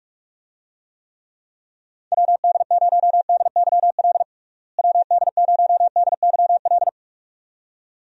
Morse code, heard as a keyed tone, about 46 wpm, 700 Hz. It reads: WD0DYL WD0DYL
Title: USA Call Signs - 6 Characters - Repeated 2x - 45wpm